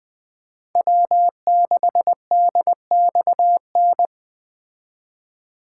Morse code, heard as a keyed tone, about 20 wpm, 700 Hz. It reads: W6DXN